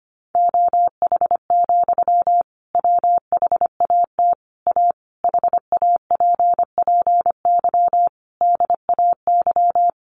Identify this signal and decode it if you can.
Morse code, heard as a keyed tone, about 25 wpm, 700 Hz. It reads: OH, WHAT A HAPPY DAY
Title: Sentences from Top 1500 Words 25wpm